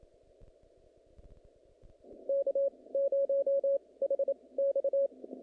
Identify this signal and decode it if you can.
Morse code, heard as a keyed tone, about 27 wpm, spaced as for 20 wpm, 550 Hz. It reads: K0HX
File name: contests / naqp-cw-k5tr-op-k5pi- -2200